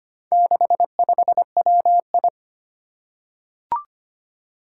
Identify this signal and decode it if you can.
Morse code, heard as a keyed tone, about 25 wpm, 700 Hz. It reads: 65WI
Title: Four Challenging Letter-Number Combinations 25wpm